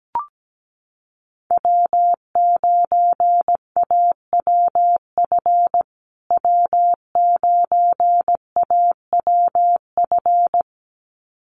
Morse code, heard as a keyed tone, about 17 wpm, 700 Hz. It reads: W9AWF W9AWF
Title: USA Call Signs Repeated 2x 17wpm